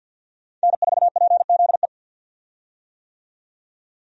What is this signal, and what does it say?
Morse code, heard as a keyed tone, about 51 wpm, 700 Hz. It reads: N4PZE
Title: USA Call Signs - 5 Characters - 50wpm